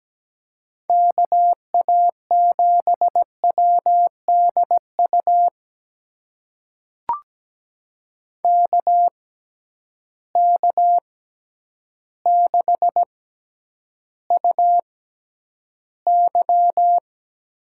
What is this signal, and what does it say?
Morse code, heard as a keyed tone, about 17 wpm, spaced as for 9 wpm, 700 Hz. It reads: KA7WDU KK6UY
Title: USA Call Signs - 6 Characters - 6x Character Spacing - 17wpm